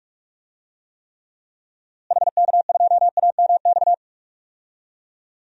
Morse code, heard as a keyed tone, about 46 wpm, 700 Hz. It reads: HK2AMX
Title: DXCC-100 Call Signs - 6 Characters - Repeated 2x - 45wpm